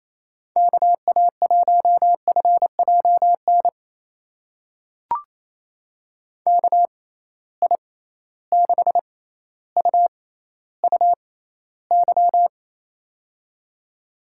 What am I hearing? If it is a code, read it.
Morse code, heard as a keyed tone, about 28 wpm, spaced as for 14 wpm, 700 Hz. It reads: KA1FJN KI6UUY
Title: USA Call Signs - 6 Characters - 6x Character Spacing - 28wpm